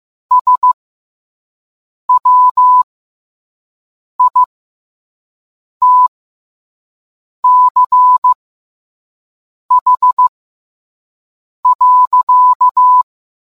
Morse code, heard as a keyed tone, about 15 wpm, spaced as for 5 wpm, 1000 Hz. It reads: SWITCH.